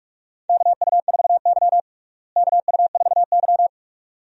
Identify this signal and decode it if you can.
Morse code, heard as a keyed tone, about 46 wpm, 700 Hz. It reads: KU4Y KU4Y
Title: USA Call Signs - 4 Characters - Repeated 2x - 45wpm